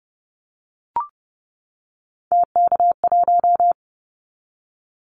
Morse code, heard as a keyed tone, about 30 wpm, 700 Hz. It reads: TK1
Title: Callsign Prefixes for Top 100 DXCC - 30wpm